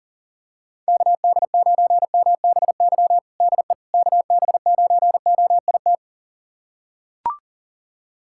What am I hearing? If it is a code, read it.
Morse code, heard as a keyed tone, about 40 wpm, 700 Hz. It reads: KD9MBY DE KB9OIT
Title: USA Callsign DE Callsign 40wpm